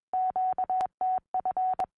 Morse code, heard as a keyed tone, about 21 wpm, 740 Hz. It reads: QTF
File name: au / sm6vtz 2m